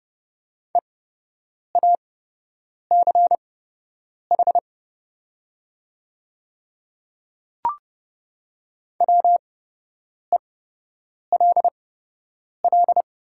Morse code, heard as a keyed tone, about 30 wpm, spaced as for 8 wpm, 700 Hz. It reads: EACH WELL
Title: 8x Character Spacing - Top 300 Words - 30wpm